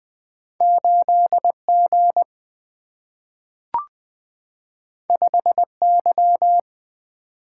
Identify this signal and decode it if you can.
Morse code, heard as a keyed tone, about 20 wpm, 700 Hz. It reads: 8 G 5 Y
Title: Single-Letter Single-Number 20wpm